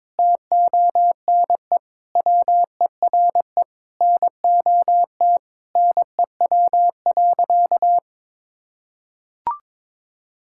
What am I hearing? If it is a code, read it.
Morse code, heard as a keyed tone, about 21 wpm, 700 Hz. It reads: TONE WERE NOT NEW.
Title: Sentences from Top 1600 Words 22wpm